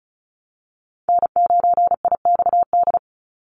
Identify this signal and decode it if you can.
Morse code, heard as a keyed tone, about 34 wpm, 700 Hz. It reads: N9IXD
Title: Call Signs Repeated 2x - 35wpm